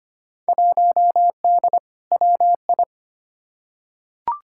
Morse code, heard as a keyed tone, about 25 wpm, 700 Hz. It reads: 1D WI
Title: Field Day Exchanges - 25wpm